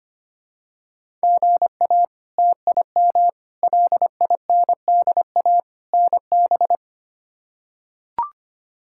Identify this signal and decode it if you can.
Morse code, heard as a keyed tone, about 25 wpm, 700 Hz. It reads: GA TIM LINDA NB
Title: K1USN SST QSO Practice - 25wpm